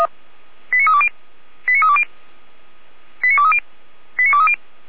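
Five-tone selective call (ZVEI), heard as quick runs of five tones, tones of about 70 ms. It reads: 89320 89320 89230 89230
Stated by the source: ZVEI five-tone call 89315 received by SDR, from the feuerwehr-orlamuende.org fire brigade server.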